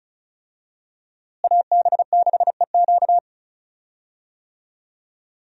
Morse code, heard as a keyed tone, about 34 wpm, 700 Hz. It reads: AB6EQ
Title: USA Call Signs - 5 Characters - Repeated 3x - 35wpm